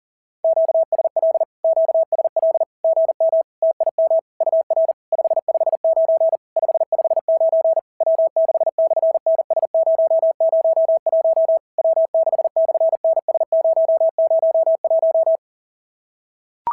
Morse code, heard as a keyed tone, about 40 wpm, 650 Hz. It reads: QSL QSL GM TIM UR 559 559 W6/NS001 W6/NS001
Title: SOTA QSO Practice 40wpm